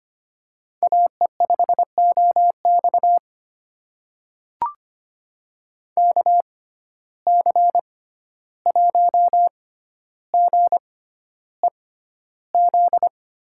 Morse code, heard as a keyed tone, about 25 wpm, spaced as for 12 wpm, 700 Hz. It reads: AE5OX KC1GEZ
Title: USA Call Signs - 6x Character Spacing - 25wpm